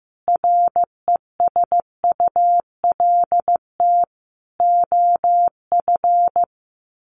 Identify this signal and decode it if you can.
Morse code, heard as a keyed tone, about 15 wpm, 700 Hz. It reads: RESULT OF
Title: Sets of 4 Words 15wpm